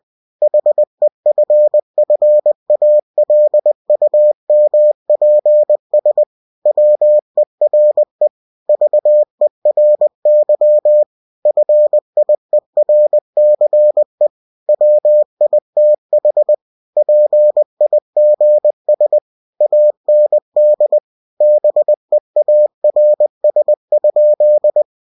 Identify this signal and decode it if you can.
Morse code, heard as a keyed tone, about 20 wpm, 600 Hz. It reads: HEFFALUMPS WERE VERY FIERCE WITH PIGS AND BEARS?
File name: cw-books / winnie-the-pooh / winnie-20wpm-page-0041